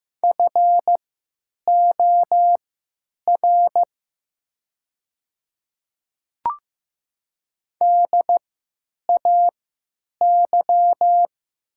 Morse code, heard as a keyed tone, about 15 wpm, spaced as for 8 wpm, 700 Hz. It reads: FOR DAY
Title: 3x Character Spacing - Top 200 Words - 15wpm